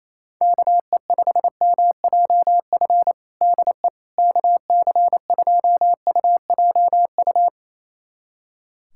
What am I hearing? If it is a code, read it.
Morse code, heard as a keyed tone, about 28 wpm, 700 Hz. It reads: KE5MJF DE KC2UJU
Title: USA Callsign DE Callsign - 6 Characters - 28wpm